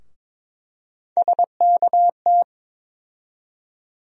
Morse code, heard as a keyed tone, about 22 wpm, 700 Hz. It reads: SKT
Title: USA Call Sign Postfixes - 3 Letters 22wpm